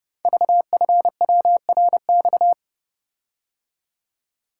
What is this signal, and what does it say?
Morse code, heard as a keyed tone, about 28 wpm, 700 Hz. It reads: VFWRX
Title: Code Groups - 5 Letters - 30wpm